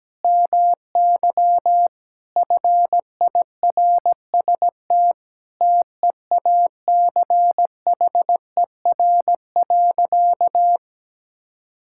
Morse code, heard as a keyed tone, about 17 wpm, 700 Hz. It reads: MY FIRST TEACHER.
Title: Sentences from Top 1000 Words - Review Part 6 - 17wpm